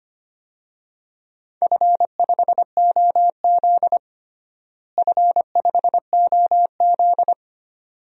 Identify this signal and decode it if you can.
Morse code, heard as a keyed tone, about 25 wpm, 700 Hz. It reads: F5OZ F5OZ
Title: DXCC-100 Call Signs - 4 Characters - Repeated 2x - 25wpm